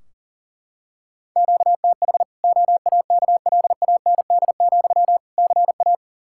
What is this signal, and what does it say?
Morse code, heard as a keyed tone, about 40 wpm, 700 Hz. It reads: QTH OAKLAND, CA
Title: K1USN SST QSO Practice 40wpm